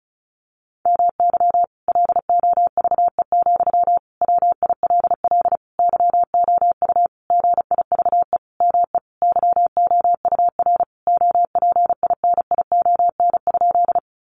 Morse code, heard as a keyed tone, about 34 wpm, 700 Hz. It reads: MY LOVE, WILL YOU GIVE ME YOUR OPINION?
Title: Sentences from Top 1200 Words 35wpm